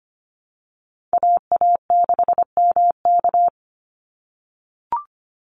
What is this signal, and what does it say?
Morse code, heard as a keyed tone, about 25 wpm, 700 Hz. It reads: AA6MK
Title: Call Signs - 25wpm